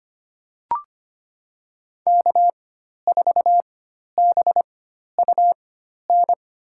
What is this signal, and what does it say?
Morse code, heard as a keyed tone, about 25 wpm, spaced as for 12 wpm, 700 Hz. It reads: K4BUN
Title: USA Call Signs - 5 Characters - 4x Character Spacing - 25wpm